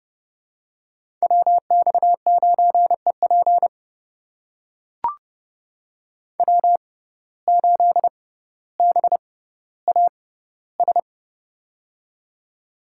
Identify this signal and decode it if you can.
Morse code, heard as a keyed tone, about 32 wpm, spaced as for 16 wpm, 700 Hz. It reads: WX9EP W8BAS
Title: USA Call Signs - 5 Characters - 6x Character Spacing - 30wpm